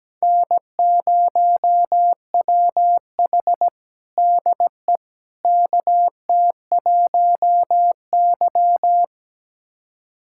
Morse code, heard as a keyed tone, about 17 wpm, 700 Hz. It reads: N0WH DE KT1Y
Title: USA Callsign DE Callsign - 4 Characters - 17wpm